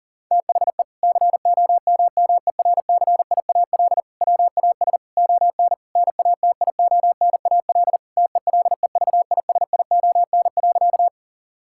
Morse code, heard as a keyed tone, about 40 wpm, 700 Hz. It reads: THE COMMERCIAL WAS ON NATIONAL TELEVISION.
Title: Sentences from Top 1000 Words - Review Part 5 - 40wpm